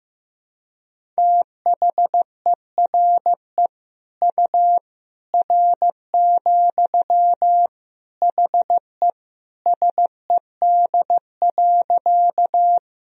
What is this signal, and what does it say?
Morse code, heard as a keyed tone, about 15 wpm, 700 Hz. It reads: THERE U R, HE SED.
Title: Sentences from Top 100 Words and Abbreviations 15wpm